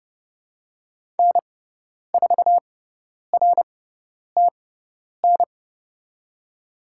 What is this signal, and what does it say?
Morse code, heard as a keyed tone, about 30 wpm, spaced as for 10 wpm, 700 Hz. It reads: N4RTN
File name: Lesson 16g - Letter c - Call Signs 30wpm